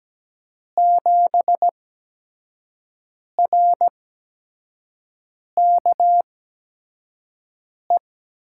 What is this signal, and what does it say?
Morse code, heard as a keyed tone, about 17 wpm, spaced as for 5 wpm, 700 Hz. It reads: 7RKE
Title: USA Call Signs - 6 Characters - 8x Character Spacing - 17wpm